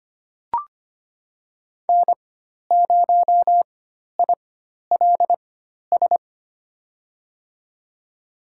Morse code, heard as a keyed tone, about 25 wpm, spaced as for 12 wpm, 700 Hz. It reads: N0ILS
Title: USA Call Signs - 5 Characters - 4x Character Spacing - 25wpm